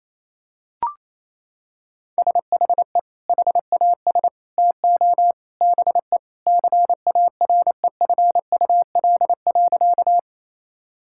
Morse code, heard as a keyed tone, about 28 wpm, 700 Hz. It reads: SHE HAS TO BE CAREFUL.
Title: Sentences from Top 1600 Words 28wpm